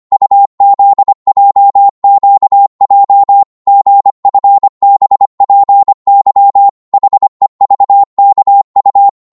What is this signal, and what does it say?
Morse code, heard as a keyed tone, about 25 wpm, 800 Hz. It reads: UZJQJ GFBPY HEVKU
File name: Morse / ltr-25wpm